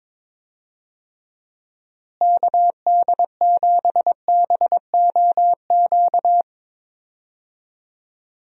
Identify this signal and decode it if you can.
Morse code, heard as a keyed tone, about 22 wpm, 700 Hz. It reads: KD7BOQ
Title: USA Call Signs - 6 Characters - Repeated 3x - 22wpm